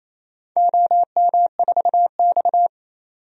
Morse code, heard as a keyed tone, about 28 wpm, 700 Hz. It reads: OM4X
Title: DXCC-100 Call Signs - 4 Characters - 28wpm